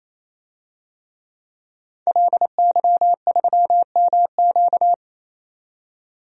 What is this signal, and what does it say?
Morse code, heard as a keyed tone, about 28 wpm, 700 Hz. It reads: LY3MQ
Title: DXCC-100 Call Signs - 5 Characters - 28wpm